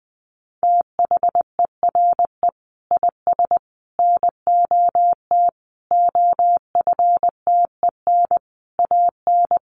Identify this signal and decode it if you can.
Morse code, heard as a keyed tone, about 20 wpm, 700 Hz. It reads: THERE IS NOT OFTEN AN
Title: Sentences from Top 1400 Words 20wpm